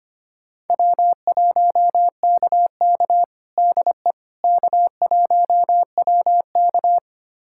Morse code, heard as a keyed tone, about 25 wpm, 700 Hz. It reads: W1KK DE K1WK
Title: USA Callsign DE Callsign - 4 Characters - 25wpm